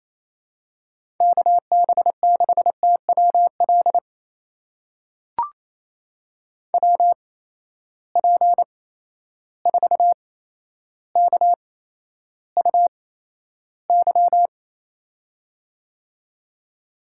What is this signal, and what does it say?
Morse code, heard as a keyed tone, about 28 wpm, spaced as for 12 wpm, 700 Hz. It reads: KB6TWL WP4KUY
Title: USA Call Signs - 6 Characters - 8x Character Spacing - 28wpm